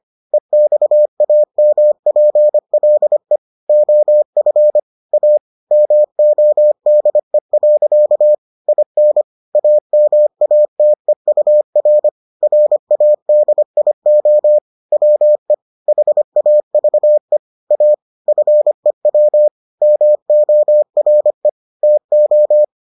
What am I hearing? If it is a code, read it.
Morse code, heard as a keyed tone, about 25 wpm, 600 Hz. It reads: EXAMPLE OF A MODE. IN AMATEUR RADIO WE HAVE A FEW MORE TO